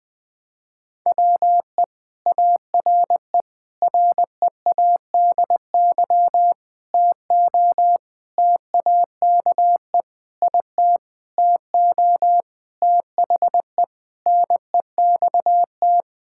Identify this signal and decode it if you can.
Morse code, heard as a keyed tone, about 20 wpm, 700 Hz. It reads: WE ARE READY TO TAKE IT TO THE NEXT
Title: Sentences with 7-Word Phrases - 20wpm